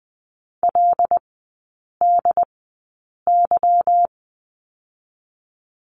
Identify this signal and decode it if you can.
Morse code, heard as a keyed tone, about 20 wpm, spaced as for 8 wpm, 690 Hz. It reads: LDY
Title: 145 Lesson 24e - Letter y - Three Characters 20wpm